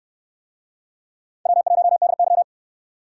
Morse code, heard as a keyed tone, about 62 wpm, 700 Hz. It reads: W1DY